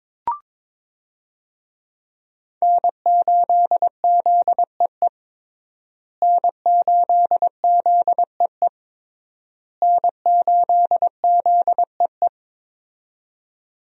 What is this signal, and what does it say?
Morse code, heard as a keyed tone, about 22 wpm, 700 Hz. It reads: N8ZEE N8ZEE N8ZEE
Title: USA Call Signs - 5 Characters - Repeated 3x - 22wpm